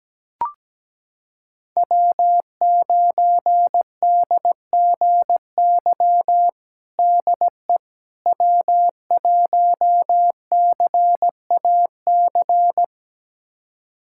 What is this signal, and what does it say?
Morse code, heard as a keyed tone, about 17 wpm, 700 Hz. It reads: W9DGY DE W1CAC
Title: USA Callsign DE Callsign - 5 Characters - 17wpm